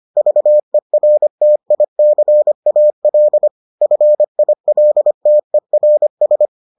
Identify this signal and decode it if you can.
Morse code, heard as a keyed tone, about 25 wpm, 600 Hz. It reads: VERTICAL FILTERS